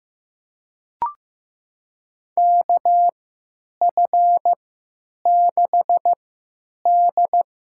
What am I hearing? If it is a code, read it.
Morse code, heard as a keyed tone, about 15 wpm, spaced as for 8 wpm, 700 Hz. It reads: KF6D
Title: USA Call Signs - 3x Character Spacing - 15wpm